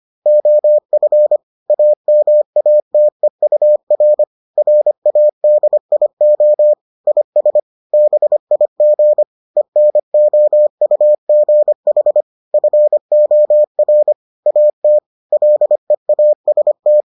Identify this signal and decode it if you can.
Morse code, heard as a keyed tone, about 25 wpm, 600 Hz. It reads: OF AMATEUR RADIO IS BIG ENOUGH FOR AT LEAST